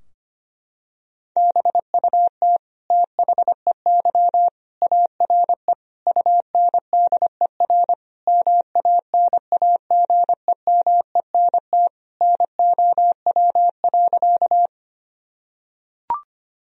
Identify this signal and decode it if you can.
Morse code, heard as a keyed tone, about 25 wpm, 700 Hz. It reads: BUT THEY ARE UNDER MANAGEMENT NOW.
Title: Sentences from Top 1000 Words - Review Part 4 - 25wpm